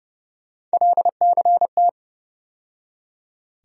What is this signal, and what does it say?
Morse code, heard as a keyed tone, about 30 wpm, 700 Hz. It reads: LCT
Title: USA Call Sign Postfixes - 3 Letters 30wpm